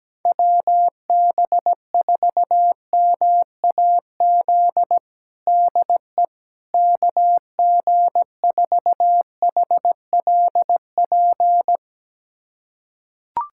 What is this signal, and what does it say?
Morse code, heard as a keyed tone, about 17 wpm, 700 Hz. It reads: WB4MAZ DE KG4HLP
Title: USA Callsign DE Callsign - 6 Characters - 17wpm